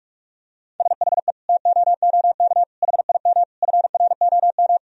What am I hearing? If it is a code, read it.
Morse code, heard as a keyed tone, about 41 wpm, 700 Hz. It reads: SHE TOOK HIM FROM
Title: Sentences from Top 1000 Words - Review Part 4 - 45wpm